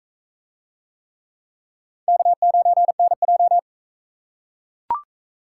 Morse code, heard as a keyed tone, about 44 wpm, 700 Hz. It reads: K9NJ